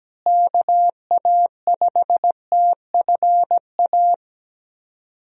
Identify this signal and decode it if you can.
Morse code, heard as a keyed tone, about 17 wpm, 700 Hz. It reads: KA5TFA
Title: USA Call Signs - 6 Characters - Repeated 2x - 17wpm